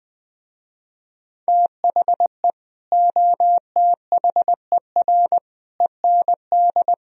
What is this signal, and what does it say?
Morse code, heard as a keyed tone, about 20 wpm, 700 Hz. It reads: THE OTHER END
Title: Speed-Racing - Sets of 5 Words 17wpm